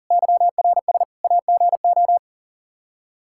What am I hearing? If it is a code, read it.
Morse code, heard as a keyed tone, about 40 wpm, 700 Hz. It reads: YRS AGO
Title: K1USN SST QSO Practice 40wpm